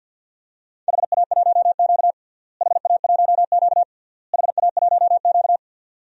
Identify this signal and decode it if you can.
Morse code, heard as a keyed tone, about 50 wpm, 700 Hz. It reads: HA1Q HA1Q HA1Q
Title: DXCC-100 Call Signs - 4 Characters - Repeated 3x - 50wpm